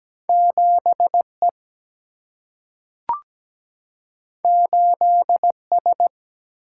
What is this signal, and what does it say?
Morse code, heard as a keyed tone, about 17 wpm, 700 Hz. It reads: 7 E 8 S
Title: Single-Letter Single-Number 17wpm